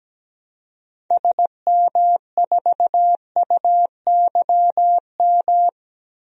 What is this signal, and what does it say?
Morse code, heard as a keyed tone, about 17 wpm, 700 Hz. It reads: SM4UYM
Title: DXCC-100 Call Signs - 6 Characters - Repeated 2x - 17wpm